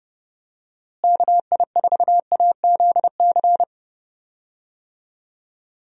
Morse code, heard as a keyed tone, about 30 wpm, 700 Hz. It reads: KI4AZC
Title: USA Call Signs - 6 Characters - Repeated 3x - 30wpm